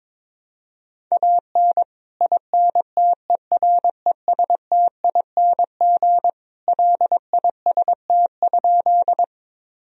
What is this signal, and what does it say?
Morse code, heard as a keyed tone, about 22 wpm, 700 Hz. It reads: AN INTERESTING LIST?
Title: Sentences from Top 1000 Words - Review Part 2 - 22wpm